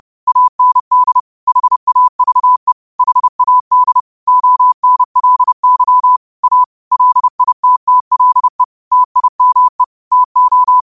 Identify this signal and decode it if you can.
Morse code, heard as a keyed tone, about 30 wpm, 1000 Hz. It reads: AND HAVE HAD ONLY A LITTLE TIME TO
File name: qotdfiles / 30wpm